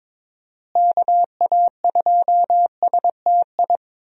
Morse code, heard as a keyed tone, about 22 wpm, 700 Hz. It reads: KA2STI
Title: USA Call Signs - 6 Characters - Repeated 2x - 22wpm